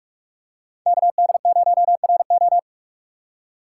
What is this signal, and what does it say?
Morse code, heard as a keyed tone, about 46 wpm, 700 Hz. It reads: KD0RO
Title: USA Call Signs - 4x Character Spacing - 45wpm